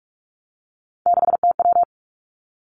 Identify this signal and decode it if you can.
Morse code, heard as a keyed tone, about 46 wpm, 700 Hz. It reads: 6TW
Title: Three Challenging Letter-Number Combinations 45wpm